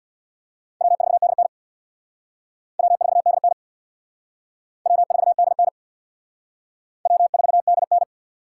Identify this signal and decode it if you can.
Morse code, heard as a keyed tone, about 61 wpm, 700 Hz. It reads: W4DN W4DN W4DN W4DN
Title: Speed Racing - USA 1x2 Call Signs 50wpm